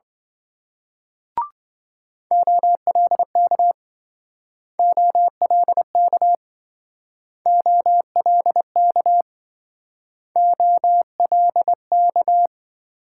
Morse code, heard as a keyed tone, about 26 wpm, 700 Hz. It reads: OLK OLK OLK OLK